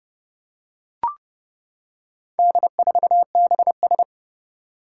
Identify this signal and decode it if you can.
Morse code, heard as a keyed tone, about 30 wpm, 700 Hz. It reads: D4BS
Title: Four Challenging Letter-Number Combinations 30wpm